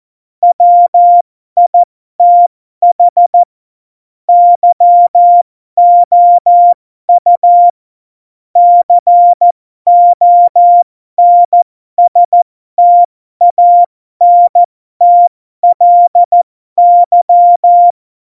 Morse code, heard as a keyed tone, about 14 wpm, spaced as for 12 wpm, 700 Hz. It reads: WITH YOU CONSTANTLY
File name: Morsekurs / Witze